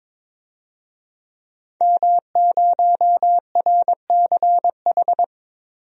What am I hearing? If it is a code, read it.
Morse code, heard as a keyed tone, about 22 wpm, 700 Hz. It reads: M0RCH
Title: DXCC-100 Call Signs - 5 Characters - 22wpm